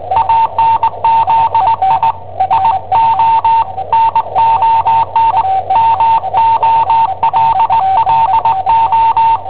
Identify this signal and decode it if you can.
Morse code, heard as a keyed tone, about 20 wpm, 910 Hz, with noise. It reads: ANZI IO NON MOLLO
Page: Cassetta esercizi preparata da I7OHP per IW5ELP